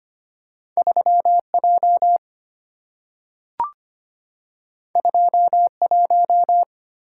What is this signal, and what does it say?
Morse code, heard as a keyed tone, about 25 wpm, 690 Hz. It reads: 3J 21
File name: Two Letter-Number Combinations - 25wpm